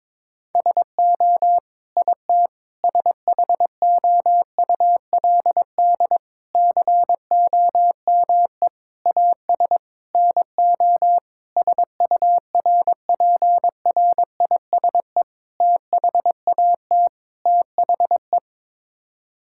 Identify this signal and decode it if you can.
Morse code, heard as a keyed tone, about 21 wpm, 700 Hz. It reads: SO IT SHOULD COME AS NO SURPRISE THAT THE
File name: Sets of 9 Words - 22wpm